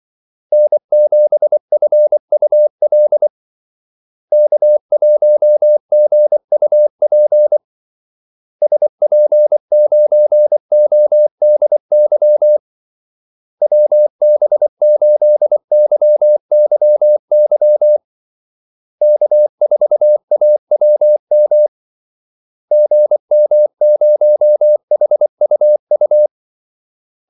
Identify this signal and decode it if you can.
Morse code, heard as a keyed tone, about 24 wpm, 600 Hz. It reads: N7FUL K1GUP SP9ODY WB8YYY K4AWM GM0HUU
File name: Callsigns-24wpm